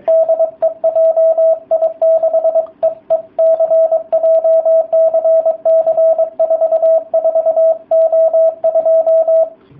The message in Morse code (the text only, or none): DE JI6EEC JCC44O2